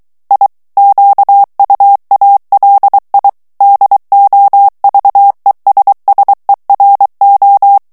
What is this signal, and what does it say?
Morse code, heard as a keyed tone, about 23 wpm, 800 Hz. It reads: I QUALI DOVESSERO
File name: ITALIEN texte clair 2 vitesse 22 mots minute